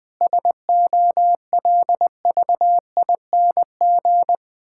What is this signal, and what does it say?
Morse code, heard as a keyed tone, about 20 wpm, 700 Hz. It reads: SOLVING
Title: Difficult Sentences with Hints - 20wpm